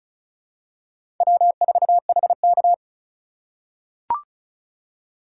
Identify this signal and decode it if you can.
Morse code, heard as a keyed tone, about 36 wpm, 700 Hz. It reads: W4HK
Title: USA Call Signs - 3x Character Spacing - 35wpm